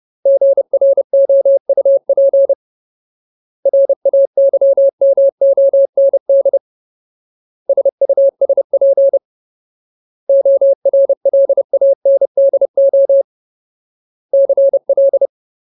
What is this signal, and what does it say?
Morse code, heard as a keyed tone, about 30 wpm, 550 Hz. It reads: GROUP RAYMOND SUSP ORLANDO CL
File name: Ultimate Word List / 30WPM Ultimate